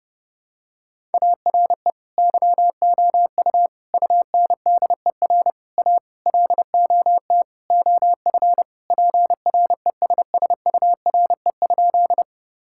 Morse code, heard as a keyed tone, about 32 wpm, 700 Hz. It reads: ARE YOU UNDER A LOT OF PRESSURE?
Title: Sentences from Top 1000 Words - Review Part 6 - 30wpm